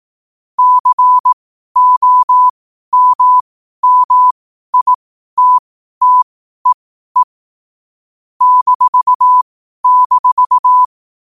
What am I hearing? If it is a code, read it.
Morse code, heard as a keyed tone, about 18 wpm, spaced as for 12 wpm, 1000 Hz. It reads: COMMITTEE --
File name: qotdfiles / 12wpm